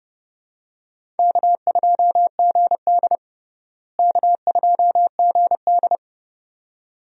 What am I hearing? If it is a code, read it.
Morse code, heard as a keyed tone, about 32 wpm, 700 Hz. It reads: K2GD K2GD
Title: USA Call Signs - 4 Characters - Repeated 2x - 30wpm